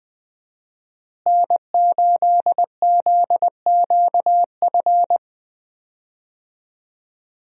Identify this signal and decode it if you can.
Morse code, heard as a keyed tone, about 21 wpm, 700 Hz. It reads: N8ZQF
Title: USA Call Signs - 5 Characters - Repeated 3x - 20wpm